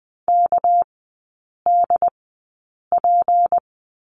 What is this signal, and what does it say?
Morse code, heard as a keyed tone, about 20 wpm, spaced as for 8 wpm, 690 Hz. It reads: KDP
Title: 198 Lesson 32e - Letter k - Three Characters 20wpm